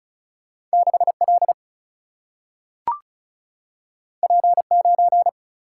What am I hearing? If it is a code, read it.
Morse code, heard as a keyed tone, about 36 wpm, 700 Hz. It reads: 6L P9
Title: Single-Letter Single-Number 35wpm